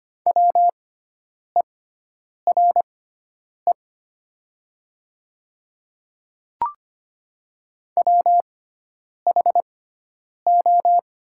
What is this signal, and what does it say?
Morse code, heard as a keyed tone, about 25 wpm, spaced as for 9 wpm, 700 Hz. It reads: WERE WHO
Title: 6x Character Spacing - Top 100 Words - 25wpm